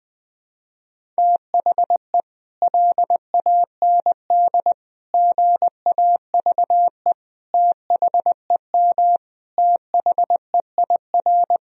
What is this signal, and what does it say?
Morse code, heard as a keyed tone, about 20 wpm, 700 Hz. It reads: THE LAND GAVE THEM THEIR
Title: Sentences from Top 1000 Words - Review Part 1 - 20wpm